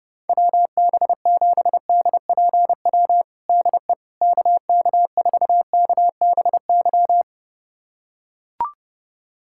Morse code, heard as a keyed tone, about 32 wpm, 700 Hz. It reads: WB7DPW DE KK4KBY
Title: USA Callsign DE Callsign - 6 Characters - 30wpm